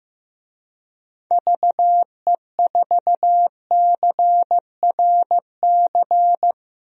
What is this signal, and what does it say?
Morse code, heard as a keyed tone, about 15 wpm, 700 Hz. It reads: VE4CRC
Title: DXCC-100 Call Signs - 6 Characters - Repeated 2x - 15wpm